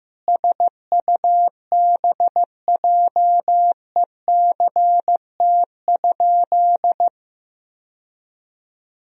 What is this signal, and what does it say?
Morse code, heard as a keyed tone, about 16 wpm, 700 Hz. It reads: SUBJECT?
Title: Sentences from Top 1000 Words - Review Part 4 - 15wpm